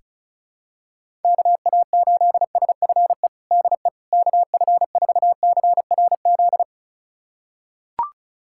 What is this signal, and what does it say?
Morse code, heard as a keyed tone, about 36 wpm, 700 Hz. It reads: KA8SFE DE KF4CRZ
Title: USA Callsign DE Callsign - 6 Characters - 35wpm